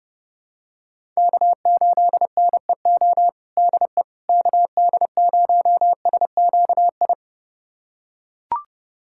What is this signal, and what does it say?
Morse code, heard as a keyed tone, about 32 wpm, 700 Hz. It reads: K8NEO DE KD0SQI
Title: USA Callsign DE Callsign 30wpm